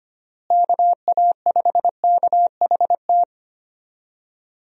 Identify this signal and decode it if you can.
Morse code, heard as a keyed tone, about 25 wpm, 700 Hz. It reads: KA5KHT
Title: USA Call Signs - 6 Characters - Repeated 3x - 25wpm